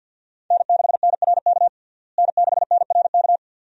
Morse code, heard as a keyed tone, about 48 wpm, 700 Hz. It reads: N6NRK N6NRK
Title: USA Call Signs - 5 Characters - Repeated 2x - 50wpm